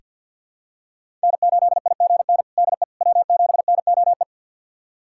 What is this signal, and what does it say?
Morse code, heard as a keyed tone, about 49 wpm, 700 Hz. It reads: N8IGN DE W7NOE
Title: USA Callsign DE Callsign - 5 Characters - 50wpm